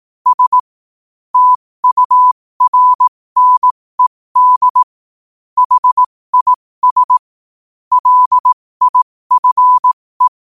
Morse code, heard as a keyed tone, about 18 wpm, spaced as for 15 wpm, 1000 Hz. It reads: S TURNED HIS LIFE
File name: qotdfiles / 15wpm